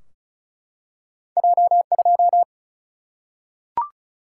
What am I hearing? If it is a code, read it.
Morse code, heard as a keyed tone, about 36 wpm, 700 Hz. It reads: J2
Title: Single-Letter Single-Number 35wpm